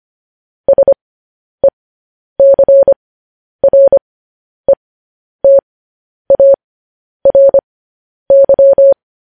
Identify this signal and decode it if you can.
Morse code, heard as a keyed tone, about 25 wpm, spaced as for 10 wpm, 550 Hz. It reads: SECRETARY
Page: News Headlines in Morse Code at 10 WPM